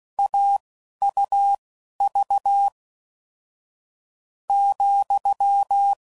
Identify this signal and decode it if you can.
Morse code, heard as a keyed tone, about 16 wpm, spaced as for 11 wpm, 800 Hz. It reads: AUV ,
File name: Difficult Combos